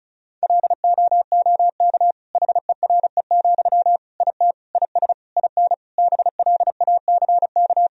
Morse code, heard as a keyed tone, about 36 wpm, 700 Hz. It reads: LOOK HERE, IT IS IN BLACK
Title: Sentences from Top 1000 Words - Review Part 2 - 35wpm